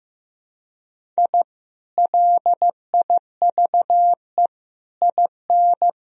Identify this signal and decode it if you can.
Morse code, heard as a keyed tone, about 15 wpm, 700 Hz. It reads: I LIVE IN
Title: Sentences from Top 1000 Words - Review Part 1 - 15wpm